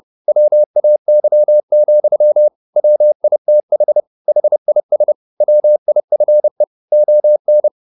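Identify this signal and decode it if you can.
Morse code, heard as a keyed tone, about 30 wpm, 600 Hz. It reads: WAY, WITH HIS WIFE ON
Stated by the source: Morse code at 30 wpm, the novel Never by Ken Follett in English, from CW med Gnister.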